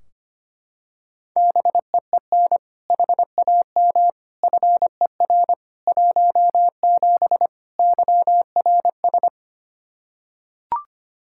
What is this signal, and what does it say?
Morse code, heard as a keyed tone, about 25 wpm, 700 Hz. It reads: BEEN HAM FER 17 YRS
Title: Practice QSO - Exchange 3 - More Info - 25wpm